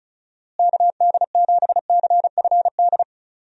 Morse code, heard as a keyed tone, about 36 wpm, 700 Hz. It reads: KD7CFD
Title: USA Call Signs - 6 Characters - Repeated 3x - 35wpm